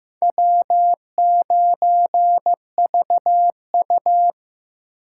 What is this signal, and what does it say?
Morse code, heard as a keyed tone, about 18 wpm, 700 Hz. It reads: W9VU
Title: Speed Racing - USA 1x2 Call Signs 15wpm